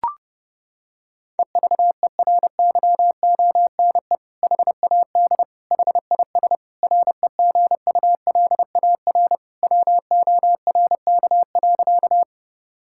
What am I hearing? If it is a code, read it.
Morse code, heard as a keyed tone, about 30 wpm, 700 Hz. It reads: EVERYONE HAD HIS REGULAR WORK.
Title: Sentences from Top 1000 Words - Review Part 6 - 30wpm